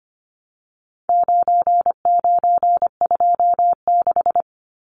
Morse code, heard as a keyed tone, about 25 wpm, 700 Hz. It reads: 9926
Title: Numbers - 4-Digit 25wpm